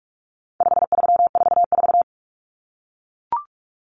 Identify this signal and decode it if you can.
Morse code, heard as a keyed tone, about 44 wpm, 700 Hz. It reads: 5344
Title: Numbers - 4-Digit 45wpm